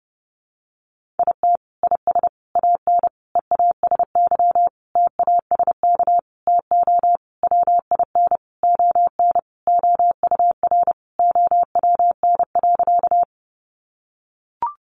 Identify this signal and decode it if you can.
Morse code, heard as a keyed tone, about 30 wpm, 700 Hz. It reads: IT IS AN EASY TASK TO WIN ON OUR OWN.
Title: Sentences from Top 1500 Words 30wpm